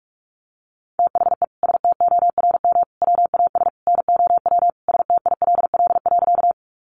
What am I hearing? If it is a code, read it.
Morse code, heard as a keyed tone, about 44 wpm, 700 Hz. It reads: THE STORM WAS NOW STILL.
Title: Sentences from Top 1500 Words 45wpm